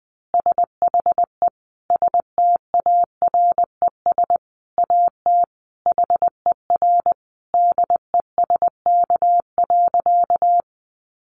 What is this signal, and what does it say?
Morse code, heard as a keyed tone, about 20 wpm, 700 Hz. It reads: SHE STARES AT HER DESK.
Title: Easy Phrases 20wpm